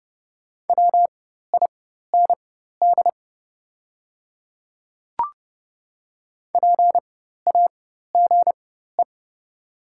Morse code, heard as a keyed tone, about 30 wpm, spaced as for 14 wpm, 690 Hz. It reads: WIND PAGE
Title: 4x Character Spacing - Top 500 Words - 30wpm